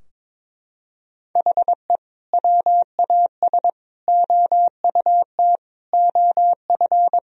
Morse code, heard as a keyed tone, about 22 wpm, 700 Hz. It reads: HE WAS OUT OF
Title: Sentences from Top 1000 Words - Review Part 4 - 22wpm